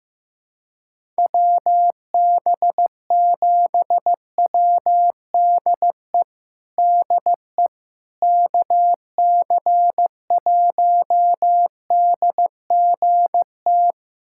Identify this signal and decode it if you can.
Morse code, heard as a keyed tone, about 15 wpm, 700 Hz. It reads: WB7WDE DE KC1DGT
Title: USA Callsign DE Callsign - 6 Characters - 15wpm